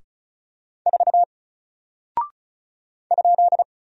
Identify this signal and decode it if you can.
Morse code, heard as a keyed tone, about 36 wpm, 700 Hz. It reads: 4 ?
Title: Single Character (Letter:Number:Punctuation:Prosign) 35wpm